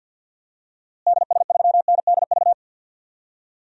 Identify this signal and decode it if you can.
Morse code, heard as a keyed tone, about 50 wpm, 700 Hz. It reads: DS3NDV
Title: DXCC-100 Call Signs - 6 Characters - Repeated 2x - 50wpm